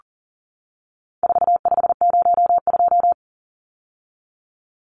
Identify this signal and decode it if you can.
Morse code, heard as a keyed tone, about 40 wpm, 700 Hz. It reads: 4502
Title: Numbers - 4-Digit 40wpm